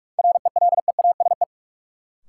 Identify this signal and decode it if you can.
Morse code, heard as a keyed tone, about 46 wpm, 700 Hz. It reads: RELEASE
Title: Sentences from Top 1100 Words - New Word Drill - 45wpm